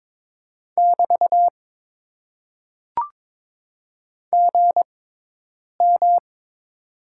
Morse code, heard as a keyed tone, about 22 wpm, spaced as for 8 wpm, 700 Hz. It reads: <BT> GM
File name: Top 90 QSO Elements - 6x Character Spacing - 22wpm